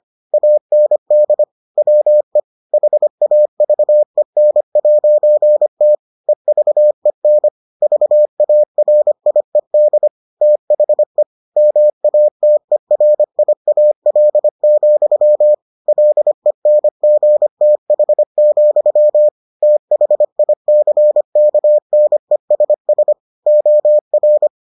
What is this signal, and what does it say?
Morse code, heard as a keyed tone, about 25 wpm, 600 Hz. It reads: AND WE HAVEN'T EVEN VARIED THE MATERIAL, LENGTH, THICKNESS OR